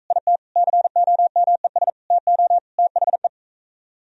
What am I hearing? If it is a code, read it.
Morse code, heard as a keyed tone, about 44 wpm, 700 Hz. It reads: IT COMES TO THE